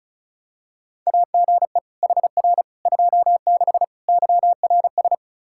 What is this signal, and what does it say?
Morse code, heard as a keyed tone, about 36 wpm, 700 Hz. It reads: AGE HR 26 YRS
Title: Practice QSO - Exchange 3 - More Info - 35wpm